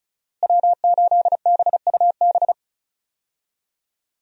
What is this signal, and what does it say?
Morse code, heard as a keyed tone, about 36 wpm, 700 Hz. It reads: W8BUB
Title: USA Call Signs - 5 Characters - Repeated 2x - 35wpm